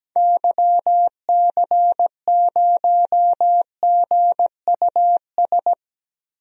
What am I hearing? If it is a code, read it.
Morse code, heard as a keyed tone, about 17 wpm, 700 Hz. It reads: YC0GUS
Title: DXCC-100 Call Signs - 6 Characters - Repeated 3x - 17wpm